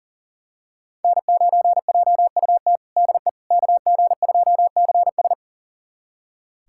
Morse code, heard as a keyed tone, about 37 wpm, 700 Hz. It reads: N9JUT DE KG2CS
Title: USA Callsign DE Callsign - 5 Characters - 40wpm